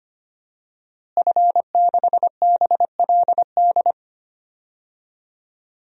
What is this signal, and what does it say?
Morse code, heard as a keyed tone, about 25 wpm, 700 Hz. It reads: F6BLD
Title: DXCC-100 Call Signs - 5 Characters - Repeated 2x - 25wpm